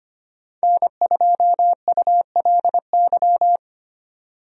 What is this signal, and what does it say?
Morse code, heard as a keyed tone, about 25 wpm, 700 Hz. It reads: N2ULY
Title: USA Call Signs - 5 Characters - 25wpm